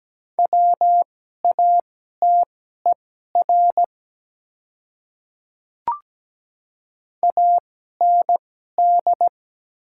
Morse code, heard as a keyed tone, about 17 wpm, spaced as for 12 wpm, 700 Hz. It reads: WATER AND
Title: 2x Character Spacing - Top 200 Words - 17wpm